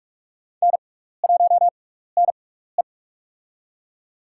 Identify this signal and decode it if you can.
Morse code, heard as a keyed tone, about 46 wpm, spaced as for 16 wpm, 700 Hz. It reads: N1NE
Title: USA Call Signs - 4 Characters - 6x Character Spacing - 45wpm